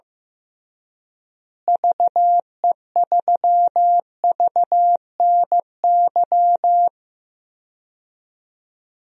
Morse code, heard as a keyed tone, about 15 wpm, 700 Hz. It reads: VE3VNY
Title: DXCC-100 Call Signs - 6 Characters - Repeated 3x - 15wpm